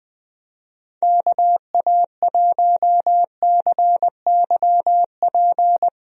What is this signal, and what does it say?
Morse code, heard as a keyed tone, about 20 wpm, 700 Hz. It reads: KA1CYP
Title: USA Call Signs - 6 Characters - 20wpm